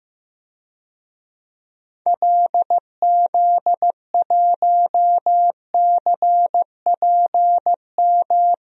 Morse code, heard as a keyed tone, about 15 wpm, 700 Hz. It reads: LZ1CPM
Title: DXCC-100 Call Signs - 6 Characters - Repeated 3x - 15wpm